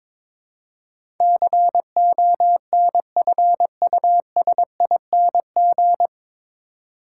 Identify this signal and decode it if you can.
Morse code, heard as a keyed tone, about 22 wpm, 700 Hz. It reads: CONFUSING
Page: Morse Code Ninja practice 22wpm